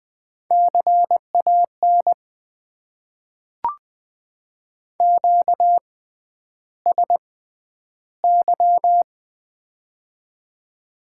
Morse code, heard as a keyed tone, about 20 wpm, spaced as for 10 wpm, 700 Hz. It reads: CAN QSY
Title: Top 90 QSO Elements - 6x Character Spacing - 20wpm